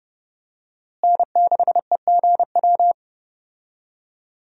Morse code, heard as a keyed tone, about 30 wpm, 700 Hz. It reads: N6EGW
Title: USA Call Signs - 5 Characters - Repeated 3x - 30wpm